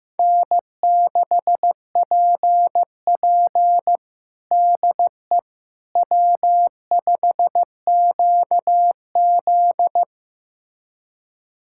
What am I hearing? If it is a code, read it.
Morse code, heard as a keyed tone, about 15 wpm, 700 Hz. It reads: N6PP DE W5QZ
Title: USA Callsign DE Callsign - 4 Characters - 15wpm